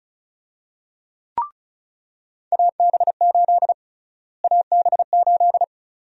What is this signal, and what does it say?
Morse code, heard as a keyed tone, about 34 wpm, 700 Hz. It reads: AB8 AB8
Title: USA Call Sign Prefixes - Repeated 2x - 35wpm